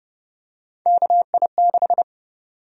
Morse code, heard as a keyed tone, about 30 wpm, 700 Hz. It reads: KI6
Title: USA Call Sign Prefixes - Repeated 2x - 30wpm